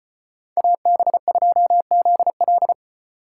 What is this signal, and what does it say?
Morse code, heard as a keyed tone, about 29 wpm, 700 Hz. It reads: AB2ZL
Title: Speed Racing - USA 2x2 Call Signs 25wpm